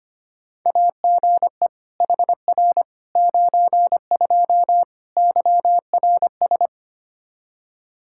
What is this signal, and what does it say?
Morse code, heard as a keyed tone, about 25 wpm, 700 Hz. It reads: AGE HR 92 YRS
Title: K1USN SST QSO Practice 25wpm